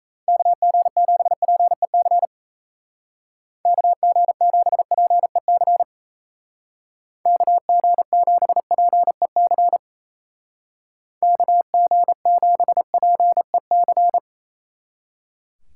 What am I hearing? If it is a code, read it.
Morse code, heard as a keyed tone, about 35 wpm, 700 Hz. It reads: KG7PEC KG7PEC KG7PEC KG7PEC
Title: Speed Racing - USA Call Signs 28wpm